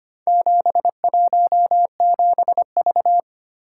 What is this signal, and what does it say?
Morse code, heard as a keyed tone, about 25 wpm, 690 Hz. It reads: 717V
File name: practice / 25wpm / Four Letter-Number Combinations - 25wpm